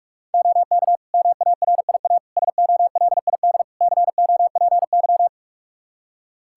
Morse code, heard as a keyed tone, about 48 wpm, 700 Hz. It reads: OK MARIA SOLID COPY
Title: Practice QSO - Exchange 3 - More Info - 45wpm